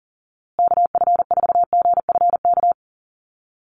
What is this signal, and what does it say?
Morse code, heard as a keyed tone, about 40 wpm, 700 Hz. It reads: KF4GFK
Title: USA Call Signs Repeated 2x 40wpm